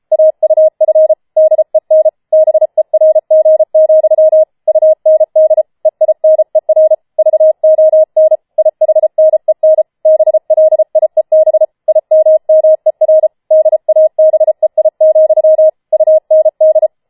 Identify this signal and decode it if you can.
Morse code, heard as a keyed tone, about 33 wpm, 630 Hz. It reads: AUF DEN BERG, UND EINER VON IHNEN BLIEB IMMER DABEI, UND
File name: Schneewittchen Teil5 34wpm